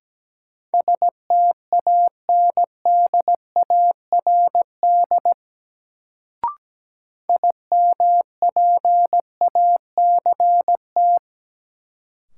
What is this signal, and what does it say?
Morse code, heard as a keyed tone, about 17 wpm, 700 Hz. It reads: STANDARD IMPACT
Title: Sentences from Top 1000 Words - Review Part 6 - New Word Drill - 17wpm